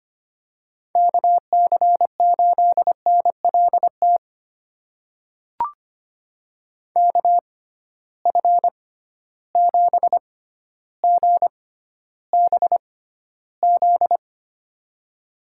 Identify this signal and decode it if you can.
Morse code, heard as a keyed tone, about 25 wpm, spaced as for 13 wpm, 700 Hz. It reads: KC8NLT KF7GBZ
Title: USA Call Signs - 6x Character Spacing - 25wpm